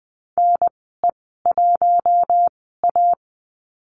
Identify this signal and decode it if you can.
Morse code, heard as a keyed tone, about 20 wpm, spaced as for 14 wpm, 700 Hz. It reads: NE1A
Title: 042 Lesson 8g - Number 1 - Call Signs 20wpm